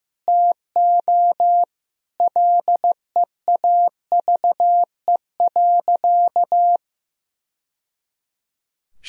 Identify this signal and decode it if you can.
Morse code, heard as a keyed tone, about 15 wpm, 700 Hz. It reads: TO LEAVE.
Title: Sentences from Top 1000 Words - Review Part 3 - 15wpm